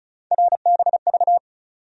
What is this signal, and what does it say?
Morse code, heard as a keyed tone, about 36 wpm, 700 Hz. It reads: RBV
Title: USA Call Sign Postfixes - 3 Letters 35wpm